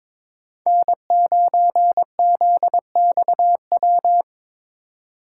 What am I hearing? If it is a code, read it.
Morse code, heard as a keyed tone, about 22 wpm, 700 Hz. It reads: N9ZXW
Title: USA Call Signs - 3x Character Spacing - 22wpm